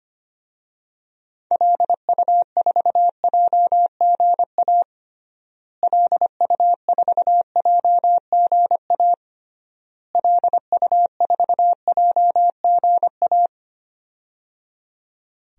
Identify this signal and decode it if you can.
Morse code, heard as a keyed tone, about 25 wpm, 700 Hz. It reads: LU4JGA LU4JGA LU4JGA
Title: DXCC-100 Call Signs - 6 Characters - Repeated 3x - 25wpm